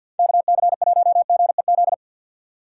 Morse code, heard as a keyed tone, about 52 wpm, 700 Hz. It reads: KC1GEZ
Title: USA Call Signs - 2x Character Spacing - 50wpm